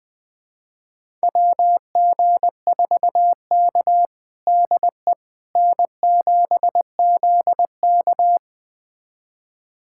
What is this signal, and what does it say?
Morse code, heard as a keyed tone, about 20 wpm, 700 Hz. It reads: WG4K DE N7ZK
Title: USA Callsign DE Callsign - 4 Characters - 20wpm